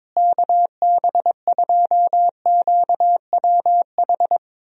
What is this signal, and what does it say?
Morse code, heard as a keyed tone, about 22 wpm, 700 Hz. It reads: KB2QWH
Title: USA Call Signs - 6 Characters - Repeated 3x - 22wpm